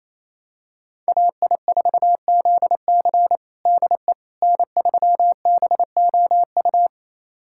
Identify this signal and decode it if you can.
Morse code, heard as a keyed tone, about 28 wpm, 700 Hz. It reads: AI4ZC DE N3BOU
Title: USA Callsign DE Callsign - 5 Characters - 28wpm